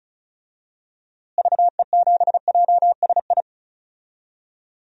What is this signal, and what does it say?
Morse code, heard as a keyed tone, about 34 wpm, 700 Hz. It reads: VE7JSI
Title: DXCC-100 Call Signs - 6 Characters - Repeated 3x - 35wpm